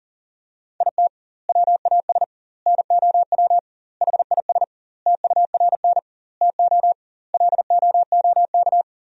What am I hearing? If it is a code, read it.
Morse code, heard as a keyed tone, about 40 wpm, 700 Hz. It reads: IT WAS NOW HIS TURN TO LOOK
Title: 2x Word-Spacing - Sentences from Top 200 Words - 40wpm